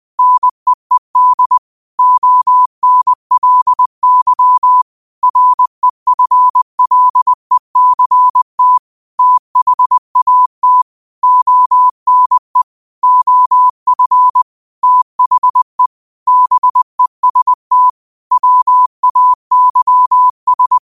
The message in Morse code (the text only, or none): NEED ONLY REFLECT THAT ONE OF THE BEST WAYS